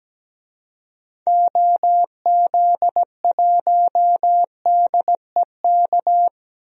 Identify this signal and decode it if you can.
Morse code, heard as a keyed tone, about 17 wpm, 700 Hz. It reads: OZ1DEK
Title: DXCC-100 Call Signs - 6 Characters - 17wpm